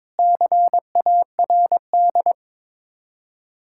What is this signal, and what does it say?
Morse code, heard as a keyed tone, about 22 wpm, 700 Hz. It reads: CARD
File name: practice / 22wpm / Sets of 7 Words - 22wpm